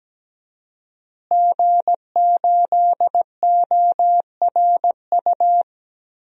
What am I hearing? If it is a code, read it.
Morse code, heard as a keyed tone, about 17 wpm, 700 Hz. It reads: G8ORU
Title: DXCC-100 Call Signs - 5 Characters - Repeated 2x - 17wpm